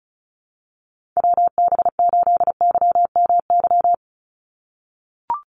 Morse code, heard as a keyed tone, about 34 wpm, 700 Hz. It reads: WB8YMY
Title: USA Call Signs 35wpm